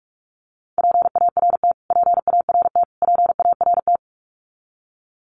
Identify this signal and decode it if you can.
Morse code, heard as a keyed tone, about 44 wpm, 700 Hz. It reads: PART PART PART
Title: Top 500 Words Repeated 45wpm